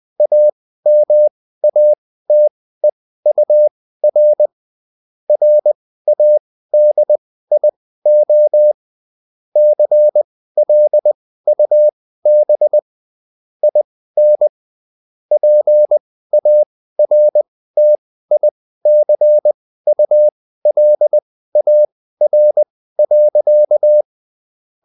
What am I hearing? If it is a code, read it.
Morse code, heard as a keyed tone, about 20 wpm, spaced as for 14 wpm, 600 Hz. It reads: AMATEUR RADIO CLUB IN PARTICULAR.